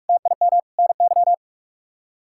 Morse code, heard as a keyed tone, about 46 wpm, 700 Hz. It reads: TIM NY
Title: K1USN SST QSO Practice - 45wpm